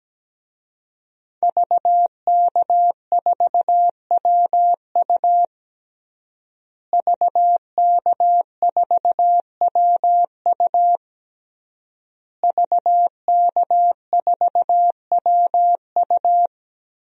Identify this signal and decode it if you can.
Morse code, heard as a keyed tone, about 17 wpm, 700 Hz. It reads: VK4WU VK4WU VK4WU
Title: DXCC-100 Call Signs - 5 Characters - Repeated 3x - 17wpm